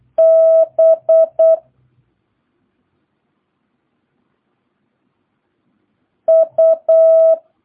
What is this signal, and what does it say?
Morse code, heard as a keyed tone, about 8 wpm, 650 Hz. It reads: B U
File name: jh8cbh / 2022cw